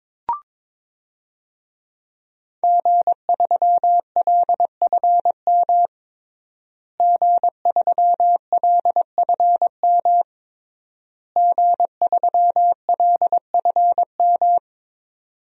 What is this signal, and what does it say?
Morse code, heard as a keyed tone, about 21 wpm, 700 Hz. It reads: G3LFM G3LFM G3LFM
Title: DXCC-100 Call Signs - 5 Characters - Repeated 3x - 22wpm